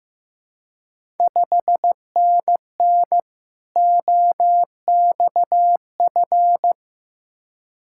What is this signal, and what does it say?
Morse code, heard as a keyed tone, about 15 wpm, 700 Hz. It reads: 5NN OXF
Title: Ontario QSO Party Exchange 15wpm